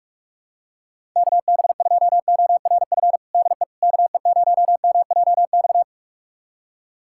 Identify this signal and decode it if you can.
Morse code, heard as a keyed tone, about 46 wpm, 690 Hz. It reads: KB2ORF DE KE0MJX
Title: USA Callsign DE Callsign - 6 Characters - 45wpm